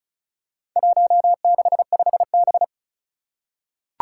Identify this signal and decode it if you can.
Morse code, heard as a keyed tone, about 36 wpm, 700 Hz. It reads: 165B
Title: Four Challenging Letter-Number Combinations 35wpm